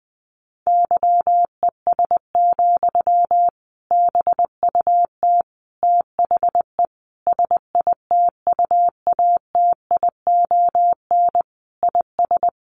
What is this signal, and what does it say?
Morse code, heard as a keyed tone, about 20 wpm, 700 Hz. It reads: YES, BUT THE SITUATION IS
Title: Sentences from Top 1200 Words 20wpm